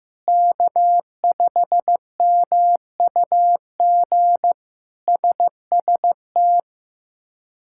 Morse code, heard as a keyed tone, about 16 wpm, 700 Hz. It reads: K5MUG SST
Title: K1USN SST QSO Practice - 15wpm